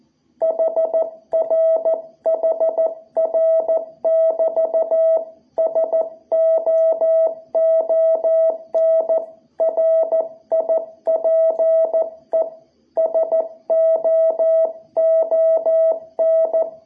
Morse code, heard as a keyed tone, about 14 wpm, 660 Hz. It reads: HR HR <BT> SOON RIPE SOON
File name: jh8cbh / 2021cw